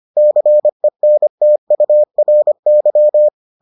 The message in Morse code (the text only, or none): CENTURY